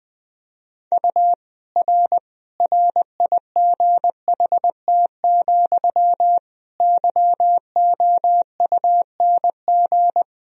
Morse code, heard as a keyed tone, about 20 wpm, 700 Hz. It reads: U R RIGHT, YOUNG
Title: Sentences from Top 400 Words and Abbreviations 20wpm